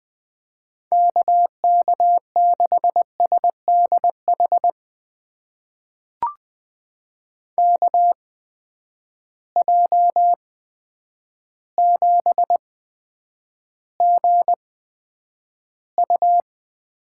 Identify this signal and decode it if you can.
Morse code, heard as a keyed tone, about 20 wpm, spaced as for 9 wpm, 700 Hz. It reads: KK6SDH KJ7GU
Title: USA Call Signs - 6 Characters - 8x Character Spacing - 20wpm